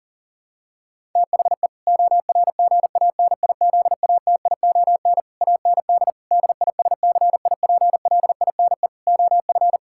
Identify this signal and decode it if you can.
Morse code, heard as a keyed tone, about 40 wpm, 700 Hz. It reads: THE ORGANIZATION AND DISCIPLINE OF
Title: Sentences from Top 1000 Words - Review Part 6 - 40wpm